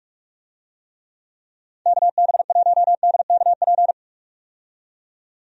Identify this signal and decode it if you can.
Morse code, heard as a keyed tone, about 46 wpm, 700 Hz. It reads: KB1DKP
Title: USA Call Signs - 6 Characters - 45wpm